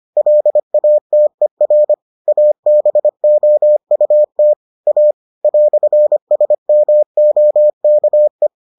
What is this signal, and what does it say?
Morse code, heard as a keyed tone, about 25 wpm, 600 Hz. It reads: LATER ABOUT A "SMOKE